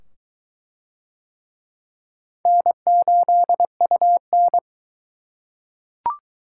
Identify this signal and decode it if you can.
Morse code, heard as a keyed tone, about 21 wpm, 700 Hz. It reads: N8UN